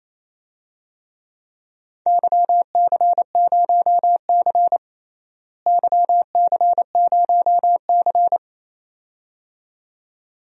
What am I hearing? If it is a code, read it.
Morse code, heard as a keyed tone, about 28 wpm, 700 Hz. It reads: YC0C YC0C
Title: DXCC-100 Call Signs - 4 Characters - Repeated 2x - 28wpm